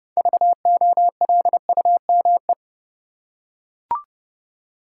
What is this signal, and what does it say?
Morse code, heard as a keyed tone, about 30 wpm, 700 Hz. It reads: VOLUME
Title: Six Letter Words - 30wpm